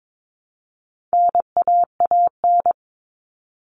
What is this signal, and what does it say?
Morse code, heard as a keyed tone, about 22 wpm, 700 Hz. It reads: NAAN
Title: All Cut Numbers - 4-Digit 22wpm